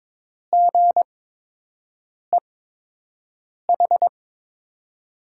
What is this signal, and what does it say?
Morse code, heard as a keyed tone, about 22 wpm, spaced as for 6 wpm, 700 Hz. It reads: GEH
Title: USA Call Signs - 5 Characters - 8x Character Spacing - 22wpm